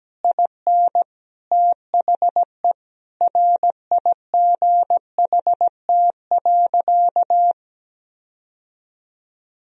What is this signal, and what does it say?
Morse code, heard as a keyed tone, about 17 wpm, 700 Hz. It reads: IN THE RIGHT.
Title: Sentences from Top 1000 Words - Review Part 1 - 17wpm